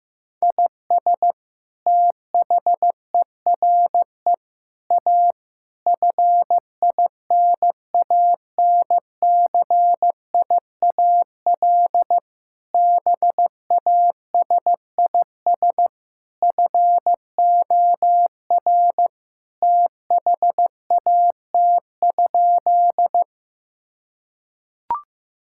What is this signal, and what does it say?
Morse code, heard as a keyed tone, about 15 wpm, 700 Hz. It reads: IS THERE A FINANCIAL BASIS FOR THAT?
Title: Sentences from Top 1000 Words - Review Part 5 - 15wpm